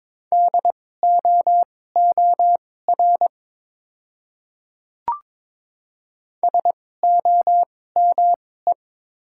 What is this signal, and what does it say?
Morse code, heard as a keyed tone, about 22 wpm, spaced as for 16 wpm, 700 Hz. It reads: DOOR SOME
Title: 2x Character Spacing - Top 500 Words - 22wpm